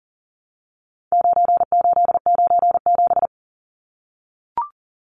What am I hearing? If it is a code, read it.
Morse code, heard as a keyed tone, about 40 wpm, 700 Hz. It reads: 9897
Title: Numbers - 4-Digit 40wpm